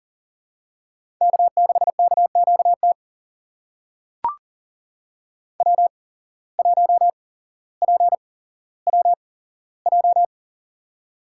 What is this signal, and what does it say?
Morse code, heard as a keyed tone, about 40 wpm, spaced as for 17 wpm, 700 Hz. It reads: K6KQT W1PWJ
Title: USA Call Signs - 5 Characters - 8x Character Spacing - 40wpm